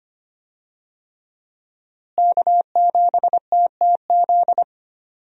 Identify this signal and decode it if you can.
Morse code, heard as a keyed tone, about 25 wpm, 700 Hz. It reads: K7TTZ
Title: USA Call Signs - 5 Characters - Repeated 3x - 25wpm